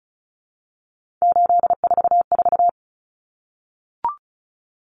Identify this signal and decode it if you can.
Morse code, heard as a keyed tone, about 34 wpm, 700 Hz. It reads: 844
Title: Common Cut Numbers - 3-Digit 35wpm